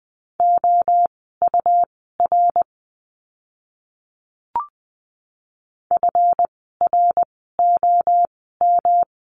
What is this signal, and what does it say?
Morse code, heard as a keyed tone, about 20 wpm, spaced as for 15 wpm, 700 Hz. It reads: OUR FROM
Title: 160 Lesson 26f - Letter g - Words 20wpm